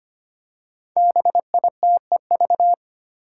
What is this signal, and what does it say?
Morse code, heard as a keyed tone, about 25 wpm, 700 Hz. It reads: BITEV
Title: Five Challenging Letter-Number Combinations 25wpm